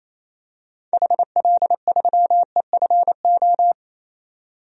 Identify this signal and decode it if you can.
Morse code, heard as a keyed tone, about 27 wpm, 700 Hz. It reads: HL3EFO
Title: DXCC-100 Call Signs - 6 Characters - 28wpm